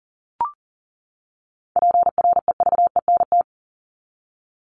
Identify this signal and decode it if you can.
Morse code, heard as a keyed tone, about 40 wpm, 700 Hz. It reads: PREVENT
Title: Top 1001-1250 Words 40wpm